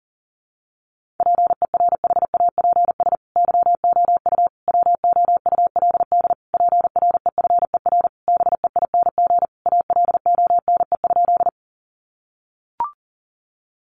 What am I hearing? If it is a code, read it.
Morse code, heard as a keyed tone, about 40 wpm, 700 Hz. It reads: PERHAPS YOU WOULD PREFER BEING ALONE?
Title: Sentences from Top 1100 Words 40wpm